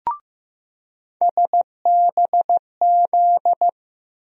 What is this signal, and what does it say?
Morse code, heard as a keyed tone, about 15 wpm, 700 Hz. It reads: SBZ
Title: USA Call Sign Postfixes - 3 Letters 15wpm